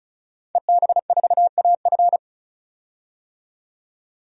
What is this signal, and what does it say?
Morse code, heard as a keyed tone, about 36 wpm, 700 Hz. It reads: EB4AF
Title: DXCC-100 Call Signs - 5 Characters - 35wpm